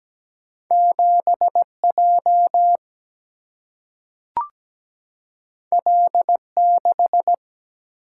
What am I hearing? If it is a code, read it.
Morse code, heard as a keyed tone, about 17 wpm, 700 Hz. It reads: 7 J L 6
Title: Single-Letter Single-Number 17wpm